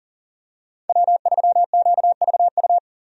Morse code, heard as a keyed tone, about 40 wpm, 700 Hz. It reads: W3QVU
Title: USA Call Signs - 5 Characters - 40wpm